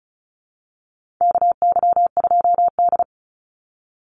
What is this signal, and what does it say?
Morse code, heard as a keyed tone, about 34 wpm, 700 Hz. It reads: KY2D
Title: Call Signs - 35wpm